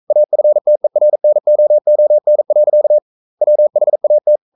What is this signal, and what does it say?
Morse code, heard as a keyed tone, about 42 wpm, 600 Hz. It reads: AFTERNOON. WHAT
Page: Romanen Never af Ken Follet på engelsk.